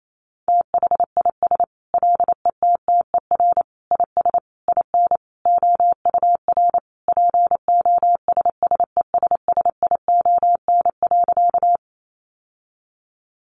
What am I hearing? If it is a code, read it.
Morse code, heard as a keyed tone, about 28 wpm, 700 Hz. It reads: THIS LETTER IS IN OUR POSSESSION.
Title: Sentences from Top 1400 Words 28wpm